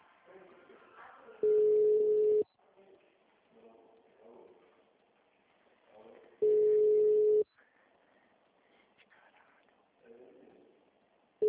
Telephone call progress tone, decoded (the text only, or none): ringback tone